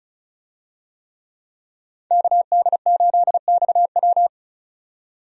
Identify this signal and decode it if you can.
Morse code, heard as a keyed tone, about 36 wpm, 700 Hz. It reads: KD8XW
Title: USA Call Signs - 5 Characters - 35wpm